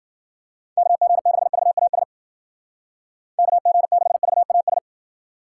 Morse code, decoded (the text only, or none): KG6VAS KG6VAS